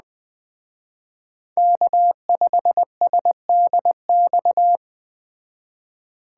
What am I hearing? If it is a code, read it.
Morse code, heard as a keyed tone, about 21 wpm, 700 Hz. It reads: K5SDX
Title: Speed Racing - USA 1x3 Call Signs 17wpm